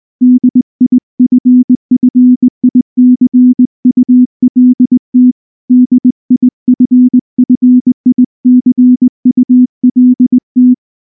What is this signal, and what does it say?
Morse code, heard as a keyed tone, about 19 wpm, 250 Hz. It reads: DIFFICULT DIFFICULT
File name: morse / morse-training-20WPM-2